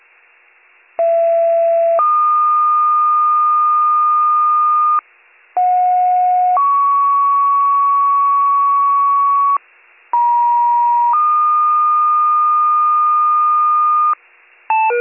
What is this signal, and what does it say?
Two-tone page, heard as a one-second tone, then a three-second tone, about 670 then 1220 Hz, about 730 then 1120 Hz, about 950 then 1220 Hz. It is heard from 1.0 to 14.1 s.